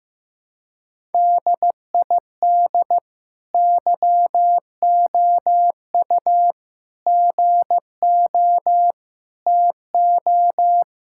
Morse code, heard as a keyed tone, about 16 wpm, 700 Hz. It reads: DID YOU GO TO
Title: Sentences from Top 1000 Words - Review Part 5 - 15wpm